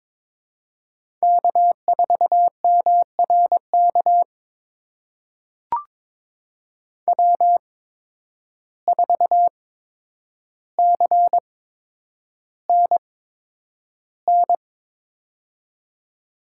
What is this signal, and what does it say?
Morse code, heard as a keyed tone, about 22 wpm, spaced as for 9 wpm, 700 Hz. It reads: K4MRK W4CNN
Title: USA Call Signs - 5 Characters - 8x Character Spacing - 22wpm